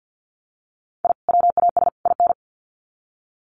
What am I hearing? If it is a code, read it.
Morse code, heard as a keyed tone, about 50 wpm, 700 Hz. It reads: I WAS IN
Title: Sets of 3 Words 50wpm